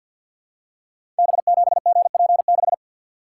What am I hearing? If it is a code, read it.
Morse code, heard as a keyed tone, about 52 wpm, 700 Hz. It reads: B7GP6
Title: Code Groups - 5 Letter:Numbers - 50wpm